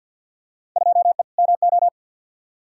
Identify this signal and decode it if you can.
Morse code, heard as a keyed tone, about 47 wpm, 700 Hz. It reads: 2E MO
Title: Field Day Exchanges - 50wpm